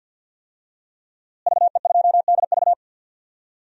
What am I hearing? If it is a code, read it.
Morse code, heard as a keyed tone, about 50 wpm, 700 Hz. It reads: VE2DV
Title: DXCC-100 Call Signs - 5 Characters - Repeated 2x - 50wpm